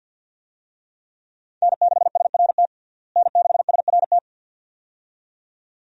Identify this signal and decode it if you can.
Morse code, heard as a keyed tone, about 50 wpm, 700 Hz. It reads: N6SRT N6SRT
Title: USA Call Signs - 5 Characters - Repeated 2x - 50wpm